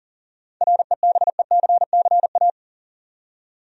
Morse code, heard as a keyed tone, about 37 wpm, 700 Hz. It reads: REBECCA
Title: USA Common Names - 40wpm